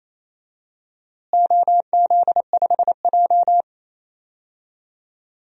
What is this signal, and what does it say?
Morse code, heard as a keyed tone, about 28 wpm, 700 Hz. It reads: OZ5J
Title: DXCC-100 Call Signs - 4 Characters - Repeated 2x - 28wpm